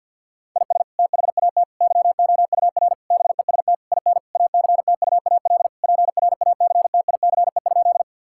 Morse code, heard as a keyed tone, about 50 wpm, 700 Hz. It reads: IS THAT YOUR BEST IN ACTUAL PRACTICE?
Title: Sentences from Top 1000 Words - Review Part 5 - 50wpm